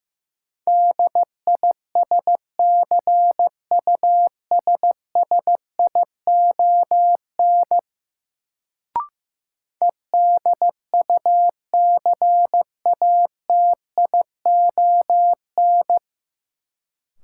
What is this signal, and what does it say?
Morse code, heard as a keyed tone, about 15 wpm, 700 Hz. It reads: DISCUSSION EDUCATION
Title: Sentences from Top 1100 Words - New Word Drill Alt - 15wpm